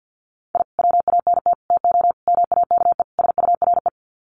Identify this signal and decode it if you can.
Morse code, heard as a keyed tone, about 49 wpm, 700 Hz. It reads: I WANT TO MAKE SURE
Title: Sets of 5 Words Encore2 50wpm